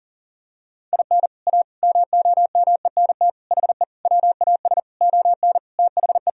Morse code, heard as a keyed tone, about 40 wpm, 700 Hz. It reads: IN A MOMENT HE WAS ON THE
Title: Sentences from Top 1000 Words - Review Part 2 - 40wpm